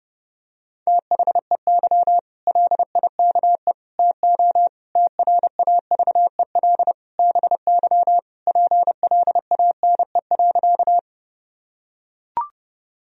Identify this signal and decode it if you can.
Morse code, heard as a keyed tone, about 30 wpm, 700 Hz. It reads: THEY LIKE TO TRAVEL BY PLANE.
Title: Sentences from Top 1000 Words - Review Part 3 - 30wpm